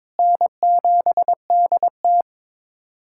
Morse code, heard as a keyed tone, about 22 wpm, 700 Hz. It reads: N7DT
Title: USA Callsign DE Callsign - 4 Characters - 22wpm